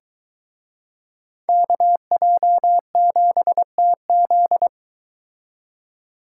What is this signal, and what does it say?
Morse code, heard as a keyed tone, about 22 wpm, 700 Hz. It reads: KJ7TZ